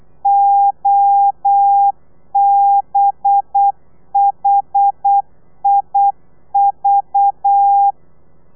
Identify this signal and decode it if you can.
Morse code, heard as a keyed tone, about 8 wpm, 800 Hz. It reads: OBHIV
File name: telegrafia / i2tfj / Lez4 35